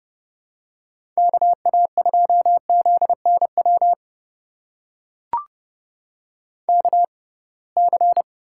Morse code, heard as a keyed tone, about 31 wpm, spaced as for 23 wpm, 700 Hz. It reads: KA2ZNW KC
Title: USA Call Signs - 6 Characters - 6x Character Spacing - 30wpm